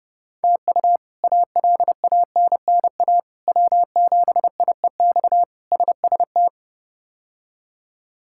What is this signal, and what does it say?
Morse code, heard as a keyed tone, about 30 wpm, 700 Hz. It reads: TU ALANNA W7IEX SST
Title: K1USN SST QSO Practice - 30wpm